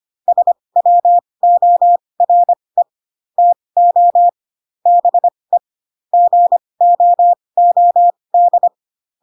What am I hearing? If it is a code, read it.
Morse code, heard as a keyed tone, about 25 wpm, spaced as for 20 wpm, 700 Hz. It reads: SWORE TO BE GOOD
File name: Pinocchio / Pinocchio ch2 15 25 ik1hge